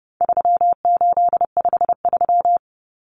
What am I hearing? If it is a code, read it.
Morse code, heard as a keyed tone, about 30 wpm, 700 Hz. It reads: 3853
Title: Common Cut Numbers - 4-Digit 30wpm